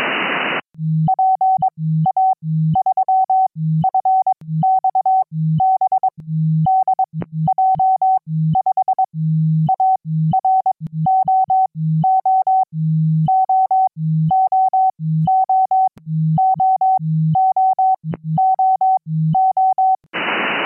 Morse code, heard as a keyed tone, about 22 wpm, spaced as for 14 wpm, 760 Hz. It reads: PA3FXB DJ5AR OOOOOOOOO
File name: QSO-PA3FXB-DJ5AR